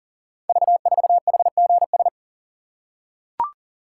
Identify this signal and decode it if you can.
Morse code, heard as a keyed tone, about 40 wpm, 690 Hz. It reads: V4HGS
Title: Code Groups - 5 Letter:Numbers - 40wpm